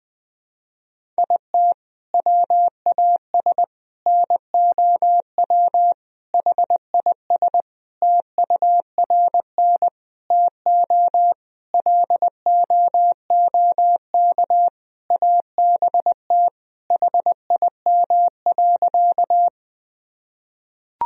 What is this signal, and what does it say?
Morse code, heard as a keyed tone, about 19 wpm, 700 Hz. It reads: IT WAS NOW HIS TURN TO LOOK ABT HIM.
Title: Sentences from Top 200 Words and Abbreviations 20wpm